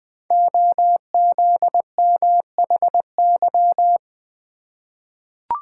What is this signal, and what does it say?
Morse code, heard as a keyed tone, about 20 wpm, 700 Hz. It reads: OZMHY
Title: Code Groups - 5 Letters - 20wpm